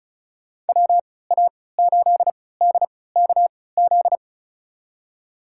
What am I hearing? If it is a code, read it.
Morse code, heard as a keyed tone, about 36 wpm, spaced as for 20 wpm, 700 Hz. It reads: WA8DKZ
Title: USA Call Signs - 6 Characters - 3x Character Spacing - 35wpm